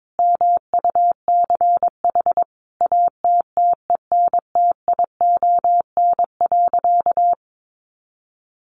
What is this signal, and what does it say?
Morse code, heard as a keyed tone, about 22 wpm, 700 Hz. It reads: MUCH ATTENTION.
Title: Sentences from Top 1000 Words 22wpm